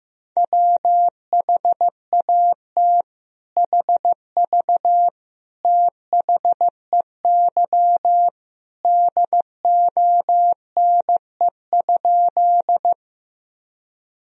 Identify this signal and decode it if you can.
Morse code, heard as a keyed tone, about 15 wpm, 700 Hz. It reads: WHAT HV THEY DONE?
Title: Sentences from Top 500 Words and Abbreviations 15wpm